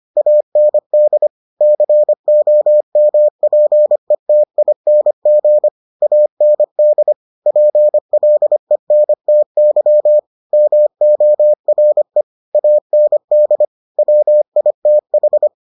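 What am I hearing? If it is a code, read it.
Morse code, heard as a keyed tone, about 25 wpm, 600 Hz. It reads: AND COMPETING AND PLENTY MORE AND WITH
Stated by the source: Morse edition of the Foundations of Amateur Radio episodes.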